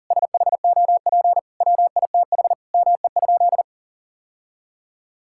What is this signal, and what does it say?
Morse code, heard as a keyed tone, about 40 wpm, 700 Hz. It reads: SHOP WITH ME?
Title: Sentences from Top 1600 Words 40wpm